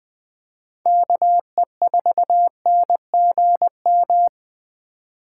Morse code, heard as a keyed tone, about 20 wpm, 700 Hz. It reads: KE4NGM
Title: USA Call Signs - 6 Characters - 4x Character Spacing - 20wpm